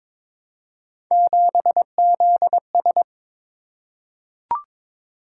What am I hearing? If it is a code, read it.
Morse code, heard as a keyed tone, about 22 wpm, 700 Hz. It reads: 7ZS
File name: practice / 22wpm / Three Letter-Number Combinations - 22wpm